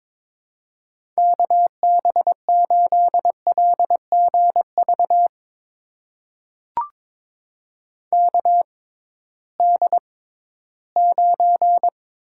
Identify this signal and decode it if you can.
Morse code, heard as a keyed tone, about 22 wpm, spaced as for 14 wpm, 700 Hz. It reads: KB8LGV KD9
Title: USA Call Signs - 6 Characters - 6x Character Spacing - 22wpm